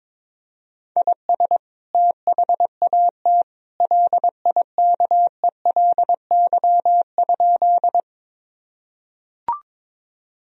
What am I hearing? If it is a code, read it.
Morse code, heard as a keyed tone, about 22 wpm, 700 Hz. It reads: IS THAT LIKELY?
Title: Sentences from Top 1000 Words - Review Part 4 - 22wpm